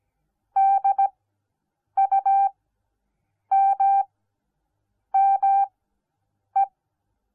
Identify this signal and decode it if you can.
Morse code, heard as a keyed tone, about 17 wpm, spaced as for 7 wpm, 780 Hz. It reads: DUMME